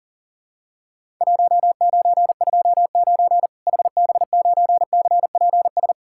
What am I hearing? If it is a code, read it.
Morse code, heard as a keyed tone, about 40 wpm, 700 Hz. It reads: 1929 HB9CPS
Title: Worked All Europe DX Contest - QTC Exchange - 40wpm